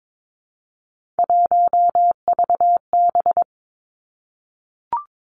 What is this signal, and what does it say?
Morse code, heard as a keyed tone, about 22 wpm, 700 Hz. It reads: 1VB
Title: Three Challenging Letter-Number Combinations 22wpm